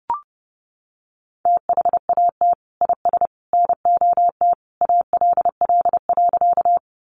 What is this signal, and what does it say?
Morse code, heard as a keyed tone, about 30 wpm, 700 Hz. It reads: THAT IS NOT ALL.
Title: Sentences from Top 600 Words 30wpm